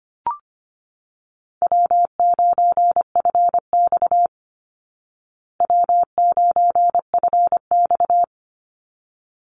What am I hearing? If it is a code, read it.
Morse code, heard as a keyed tone, about 25 wpm, 700 Hz. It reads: W9FX W9FX
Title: Easy Call Signs Repeated 2x - 25wpm